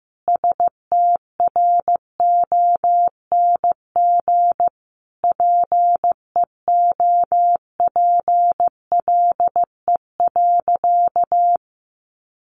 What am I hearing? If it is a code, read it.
Morse code, heard as a keyed tone, about 15 wpm, 700 Hz. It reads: STRONG PEOPLE.
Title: Sentences from Top 1400 Words 15wpm